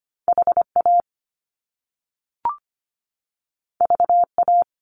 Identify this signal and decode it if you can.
Morse code, heard as a keyed tone, about 25 wpm, 700 Hz. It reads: HA VA
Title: Top Two Letter Combinations in Words 25wpm